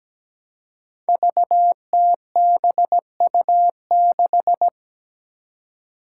Morse code, heard as a keyed tone, about 17 wpm, 700 Hz. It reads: VTBU6
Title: Five Challenging Letter-Number Combinations 17wpm